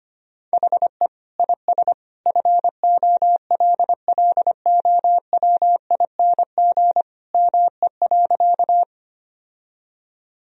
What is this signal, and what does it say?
Morse code, heard as a keyed tone, about 25 wpm, 700 Hz. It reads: HE IS FOLLOWING ME.
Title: Sentences from Top 1000 Words - Review Part 4 - 25wpm